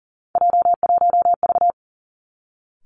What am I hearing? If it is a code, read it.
Morse code, heard as a keyed tone, about 39 wpm, 700 Hz. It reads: J1V
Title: Three Challenging Letter-Number Combinations 40wpm